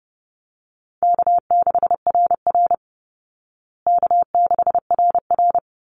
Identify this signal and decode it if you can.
Morse code, heard as a keyed tone, about 30 wpm, 700 Hz. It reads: K6RR K6RR
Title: Call Signs Repeated 3x - 30wpm